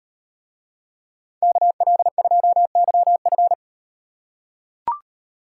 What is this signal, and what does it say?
Morse code, heard as a keyed tone, about 38 wpm, 700 Hz. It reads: KL2YF